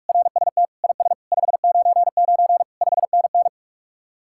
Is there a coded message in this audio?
Yes: Morse code from 0.1 to 3.5 s, heard as a keyed tone, about 48 wpm, 700 Hz. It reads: RST IS 599 5NN